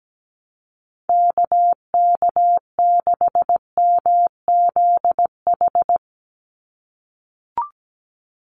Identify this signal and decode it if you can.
Morse code, heard as a keyed tone, about 17 wpm, 700 Hz. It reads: KK6MZH
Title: USA Call Signs 17wpm